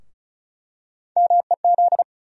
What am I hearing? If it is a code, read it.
Morse code, heard as a keyed tone, about 34 wpm, 700 Hz. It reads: MEZ
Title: USA Call Sign Postfixes - 3 Letters 35wpm